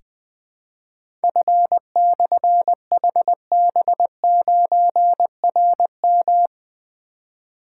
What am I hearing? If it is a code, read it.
Morse code, heard as a keyed tone, about 20 wpm, 700 Hz. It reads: F/HB9RM
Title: Difficult Call Signs 20wpm